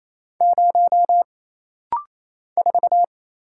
Morse code, heard as a keyed tone, about 28 wpm, 700 Hz. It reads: 0 4
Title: Single Letter-Number Alt - Rapid-Fire 28wpm